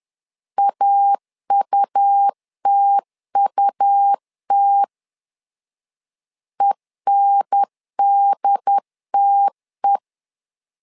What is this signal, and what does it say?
Morse code, heard as a keyed tone, about 11 wpm, 800 Hz. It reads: AUTUT ENDTE
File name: LEZIONI / 3 MWRO / 3 M 10wpm 1828